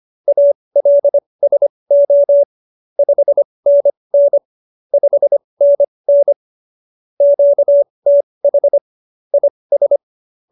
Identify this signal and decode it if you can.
Morse code, heard as a keyed tone, about 25 wpm, spaced as for 20 wpm, 570 Hz. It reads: ALSO 5NN 5NN QTH IS
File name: QSO309 20